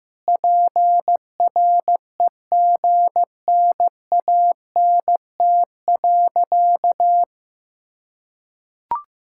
Morse code, heard as a keyed tone, about 16 wpm, 700 Hz. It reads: PREGNANT.
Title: Sentences from Top 1000 Words - Review Part 6 - 15wpm